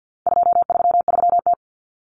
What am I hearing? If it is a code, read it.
Morse code, heard as a keyed tone, about 50 wpm, 700 Hz. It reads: 233T
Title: Common Cut Numbers - 4-Digit 50wpm